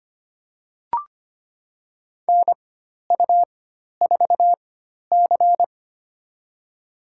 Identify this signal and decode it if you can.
Morse code, heard as a keyed tone, about 25 wpm, spaced as for 12 wpm, 700 Hz. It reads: NU4C
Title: USA Call Signs - 4 Characters - 4x Character Spacing - 25wpm